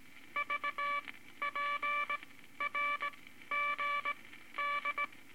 Morse code, heard as a keyed tone, about 19 wpm, spaced as for 13 wpm, 1160 Hz, with noise. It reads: VPRGD
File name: Morse Code Records / 18 B LCM 165 Ten Words Per Minute